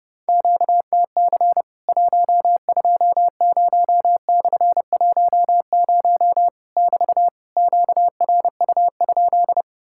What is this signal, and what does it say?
Morse code, heard as a keyed tone, about 30 wpm, 700 Hz. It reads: QTC 120/10 <BT> QRU?
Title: Worked All Europe DX Contest - QTC Exchange - 30wpm